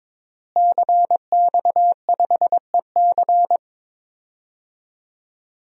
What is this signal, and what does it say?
Morse code, heard as a keyed tone, about 22 wpm, 700 Hz. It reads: CX5EC
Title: DXCC-100 Call Signs - 5 Characters - Repeated 3x - 22wpm